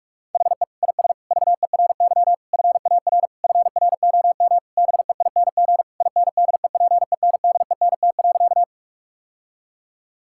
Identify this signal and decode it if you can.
Morse code, heard as a keyed tone, about 46 wpm, 700 Hz. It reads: HE IS VERY FAR FROM BEING INDEPENDENT.
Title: Sentences from Top 1000 Words - Review Part 6 - 45wpm